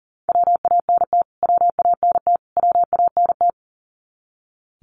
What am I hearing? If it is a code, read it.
Morse code, heard as a keyed tone, about 40 wpm, 700 Hz. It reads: WANT WANT WANT
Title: Top 200 Words Repeated 40wpm